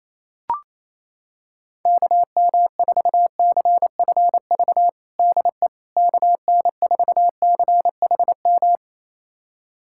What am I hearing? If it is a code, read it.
Morse code, heard as a keyed tone, about 28 wpm, 700 Hz. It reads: KM4CFV DE KN4CHM
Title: USA Callsign DE Callsign - 6 Characters - 28wpm